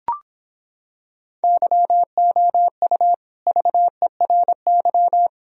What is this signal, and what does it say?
Morse code, heard as a keyed tone, about 26 wpm, 700 Hz. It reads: YOU VERY